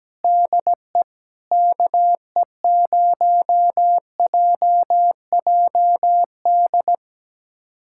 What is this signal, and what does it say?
Morse code, heard as a keyed tone, about 17 wpm, 700 Hz. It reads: DE KE0JJD
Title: USA Callsign DE Callsign 17wpm